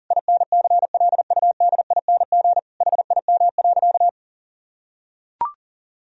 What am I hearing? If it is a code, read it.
Morse code, heard as a keyed tone, about 37 wpm, 700 Hz. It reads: INCLUDING HIM.
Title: Sentences from Top 1000 Words - Review Part 4 - 40wpm